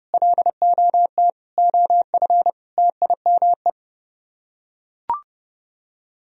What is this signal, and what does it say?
Morse code, heard as a keyed tone, about 29 wpm, 700 Hz. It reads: LOT OF TIME